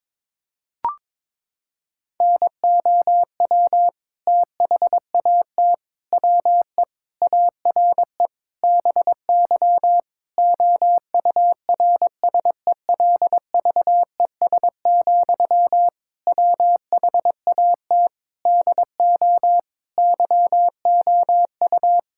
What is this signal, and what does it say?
Morse code, heard as a keyed tone, about 23 wpm, 700 Hz. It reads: NOW THAT WE ARE BY OURSELVES, WHAT DO YOU
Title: Sentences from Top 1600 Words 22wpm